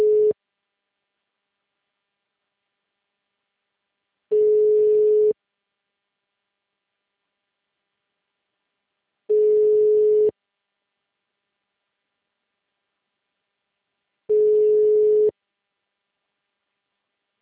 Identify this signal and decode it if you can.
Ringback tone (Europe, 425 Hz).